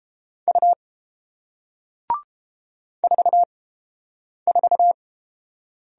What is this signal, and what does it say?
Morse code, heard as a keyed tone, about 30 wpm, spaced as for 8 wpm, 700 Hz. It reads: U 44